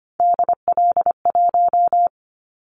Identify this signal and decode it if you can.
Morse code, heard as a keyed tone, about 25 wpm, 700 Hz. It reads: DL1
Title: Callsign Prefixes for Top 50 DXCC - 25wpm